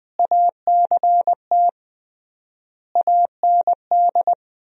Morse code, heard as a keyed tone, about 21 wpm, 700 Hz. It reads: ACT AND
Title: 3x Word-Spacing - Sentences from Top 500 Words - 20wpm